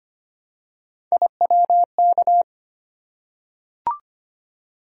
Morse code, heard as a keyed tone, about 25 wpm, 700 Hz. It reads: IWK
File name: Three Letter-Number Combinations - 25wpm